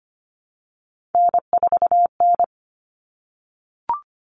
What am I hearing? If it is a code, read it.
Morse code, heard as a keyed tone, about 25 wpm, 700 Hz. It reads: N4N
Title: Common Cut Numbers - 3-Digit 25wpm